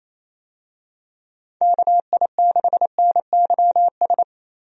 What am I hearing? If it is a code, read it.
Morse code, heard as a keyed tone, about 28 wpm, 700 Hz. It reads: KI6NYS
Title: USA Call Signs - 6 Characters - Repeated 3x - 28wpm